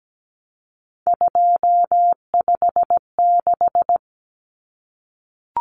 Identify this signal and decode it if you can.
Morse code, heard as a keyed tone, about 17 wpm, 690 Hz. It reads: 256
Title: Common Cut Numbers - 3-Digit 17wpm